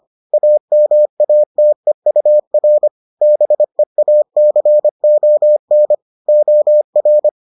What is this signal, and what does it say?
Morse code, heard as a keyed tone, about 25 wpm, 600 Hz. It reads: AMATEUR BEACON OR